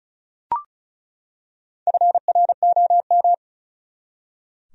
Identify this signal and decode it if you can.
Morse code, heard as a keyed tone, about 36 wpm, 700 Hz. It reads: FROM
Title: Top 100 Words 35wpm